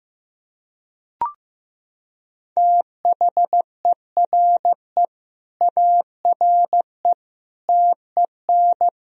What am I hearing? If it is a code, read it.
Morse code, heard as a keyed tone, about 16 wpm, 700 Hz. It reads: THERE ARE TEN
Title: Sentences from Top 1000 Words - Review Part 2 - 15wpm